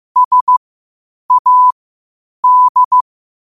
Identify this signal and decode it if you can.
Morse code, heard as a keyed tone, about 15 wpm, spaced as for 8 wpm, 1000 Hz. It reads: SAD
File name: qotdfiles / 8wpm